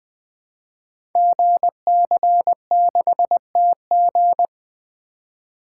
Morse code, heard as a keyed tone, about 20 wpm, 700 Hz. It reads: GC6TG
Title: Code Groups - 5 Letter:Numbers - 20wpm